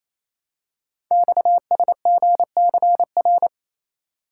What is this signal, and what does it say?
Morse code, heard as a keyed tone, about 29 wpm, 700 Hz. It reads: XSGCR
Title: Code Groups - 5 Letters - 28wpm